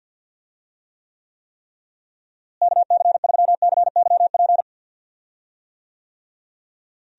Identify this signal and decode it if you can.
Morse code, heard as a keyed tone, about 48 wpm, 700 Hz. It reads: KC3CYP
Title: USA Call Signs - 6 Characters - Repeated 3x - 50wpm